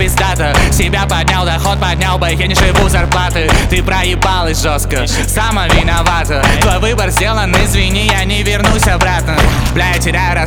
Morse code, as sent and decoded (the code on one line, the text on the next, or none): none